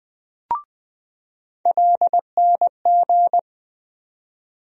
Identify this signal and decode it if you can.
Morse code, heard as a keyed tone, about 20 wpm, 700 Hz. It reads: LNG
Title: USA Call Sign Postfixes - 3 Letters 20wpm